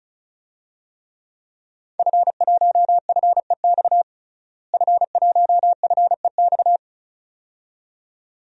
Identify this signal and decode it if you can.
Morse code, heard as a keyed tone, about 34 wpm, 700 Hz. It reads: F1FEX F1FEX
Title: DXCC-100 Call Signs - 5 Characters - Repeated 2x - 35wpm